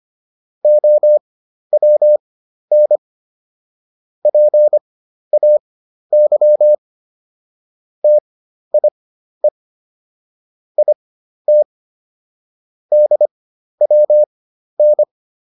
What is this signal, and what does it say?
Morse code, heard as a keyed tone, about 25 wpm, spaced as for 12 wpm, 600 Hz. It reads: OWN PAY TIE IT DWN